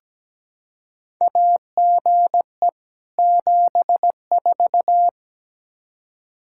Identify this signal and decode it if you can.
Morse code, heard as a keyed tone, about 17 wpm, 700 Hz. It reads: AGE 74
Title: K1USN SST QSO Practice 17wpm